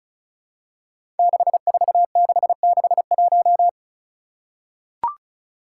Morse code, heard as a keyed tone, about 35 wpm, 700 Hz. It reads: 64661
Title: Five Challenging Letter-Number Combinations 35wpm